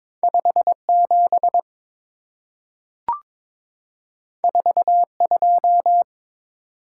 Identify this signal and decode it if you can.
Morse code, heard as a keyed tone, about 22 wpm, 700 Hz. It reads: 57 42
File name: practice / 22wpm / Two Letter-Number Combinations - 22wpm